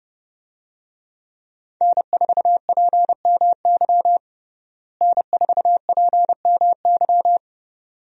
Morse code, heard as a keyed tone, about 30 wpm, 700 Hz. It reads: N4PMY N4PMY
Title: USA Call Signs - 5 Characters - Repeated 2x - 30wpm